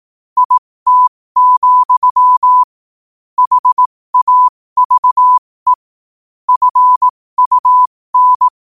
Morse code, heard as a keyed tone, about 18 wpm, spaced as for 15 wpm, 1000 Hz. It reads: IT, HAVE FUN
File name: 15wpm